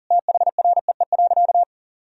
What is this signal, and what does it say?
Morse code, heard as a keyed tone, about 40 wpm, 700 Hz. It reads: THREE.
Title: Sentences from Top 1000 Words - Review Part 4 - 40wpm